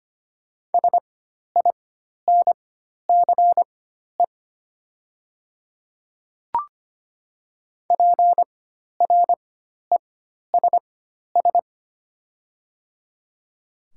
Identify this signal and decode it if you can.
Morse code, heard as a keyed tone, about 25 wpm, spaced as for 12 wpm, 700 Hz. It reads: SINCE PRESS
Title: 4x Character Spacing - Top 300 Words - 25wpm